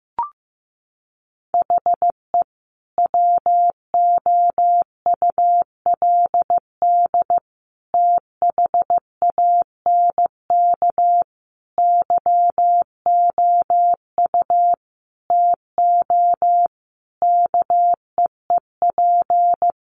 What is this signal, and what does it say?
Morse code, heard as a keyed tone, about 15 wpm, 700 Hz. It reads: HE WOULD THANK YOU TO KEEP
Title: Sentences from Top 1100 Words 15wpm